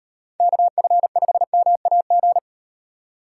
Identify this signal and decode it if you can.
Morse code, heard as a keyed tone, about 38 wpm, 700 Hz. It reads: KF5MAG